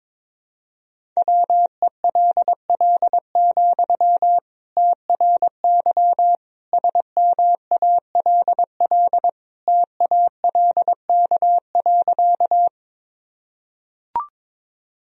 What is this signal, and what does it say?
Morse code, heard as a keyed tone, about 22 wpm, 700 Hz. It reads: WELL, TRY SMALL TALK.
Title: Sentences from Top 1000 Words - Review Part 1 - 22wpm